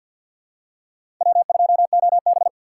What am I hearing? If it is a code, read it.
Morse code, heard as a keyed tone, about 49 wpm, 700 Hz. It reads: W2OB
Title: USA Call Signs - 6x Character Spacing - 50wpm